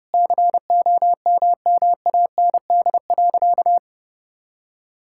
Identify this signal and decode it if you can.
Morse code, heard as a keyed tone, about 30 wpm, 700 Hz. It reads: COMMAND.
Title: Sentences from Top 1600 Words 30wpm